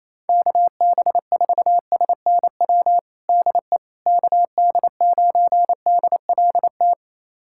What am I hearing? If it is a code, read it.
Morse code, heard as a keyed tone, about 28 wpm, 700 Hz. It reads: KB4SNW DE KD9DLT
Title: USA Callsign DE Callsign 28wpm